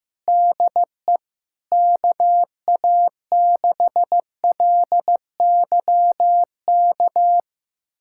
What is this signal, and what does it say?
Morse code, heard as a keyed tone, about 15 wpm, 700 Hz. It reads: DE KA6LYK
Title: USA Callsign DE Callsign 15wpm